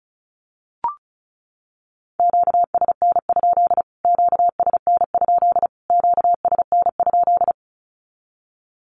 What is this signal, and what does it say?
Morse code, heard as a keyed tone, about 34 wpm, 700 Hz. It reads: QSN? QSN? QSN?
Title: Q-Codes Repeated 3x 35wpm